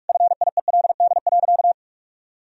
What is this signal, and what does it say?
Morse code, heard as a keyed tone, about 41 wpm, 700 Hz. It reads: FIELD.
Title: Sentences from Top 1000 Words - Review Part 2 - 45wpm